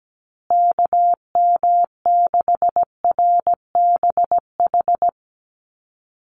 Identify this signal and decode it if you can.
Morse code, heard as a keyed tone, about 17 wpm, 700 Hz. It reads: KM6RBH
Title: USA Call Signs Repeated 3x 17wpm